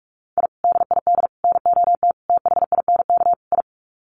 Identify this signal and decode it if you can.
Morse code, heard as a keyed tone, about 44 wpm, 700 Hz. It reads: I DID NOT THINK I
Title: Sets of 5 Words Encore 45wpm